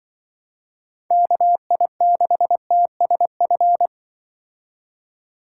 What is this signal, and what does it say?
Morse code, heard as a keyed tone, about 25 wpm, 700 Hz. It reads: KI6TSF
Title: Speed Racing - USA 2x3 Call Signs 20wpm